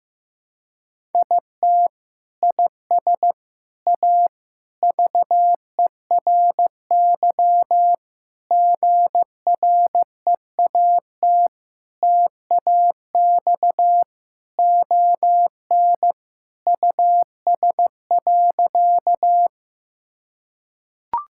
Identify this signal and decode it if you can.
Morse code, heard as a keyed tone, about 16 wpm, 700 Hz. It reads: IT IS A VERY GREAT TAX ON US.
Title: Sentences from Top 1000 Words - Review Part 5 - 15wpm